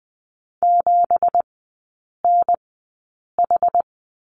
Morse code, heard as a keyed tone, about 20 wpm, spaced as for 8 wpm, 700 Hz. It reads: 7NH
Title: 172 Lesson 28e - Number 9 - Three Characters 20wpm